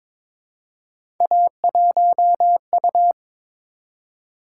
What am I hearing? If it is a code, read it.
Morse code, heard as a keyed tone, about 22 wpm, 700 Hz. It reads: A1U
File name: practice / 22wpm / Three Letter-Number Combinations - 22wpm